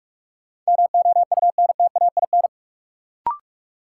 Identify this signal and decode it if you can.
Morse code, heard as a keyed tone, about 42 wpm, 700 Hz. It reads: MOUNTAIN
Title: Sentences from Top 1000 Words - Review Part 1 - New Word Drill - 45wpm